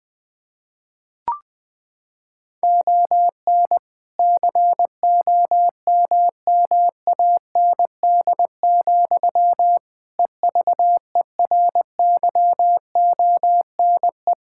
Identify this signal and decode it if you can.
Morse code, heard as a keyed tone, about 20 wpm, 700 Hz. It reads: ON COMMAND, EVERYONE
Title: Sentences from Top 1600 Words 20wpm